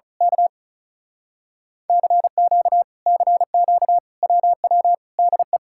Morse code, heard as a keyed tone, about 33 wpm, 700 Hz. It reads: K CQ CQ WW DE
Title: Speed-Racing Common QSO Phrases - 35wpm